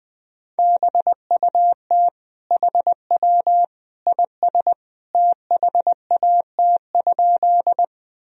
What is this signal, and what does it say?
Morse code, heard as a keyed tone, about 20 wpm, 700 Hz. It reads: BUT HW IS THAT?
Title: Sentences from Top 100 Words and Abbreviations 20wpm